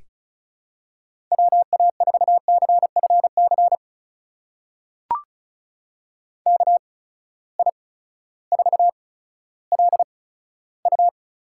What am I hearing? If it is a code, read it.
Morse code, heard as a keyed tone, about 36 wpm, spaced as for 16 wpm, 700 Hz. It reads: WA4CFC KI4LU
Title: USA Call Signs - 6 Characters - 8x Character Spacing - 35wpm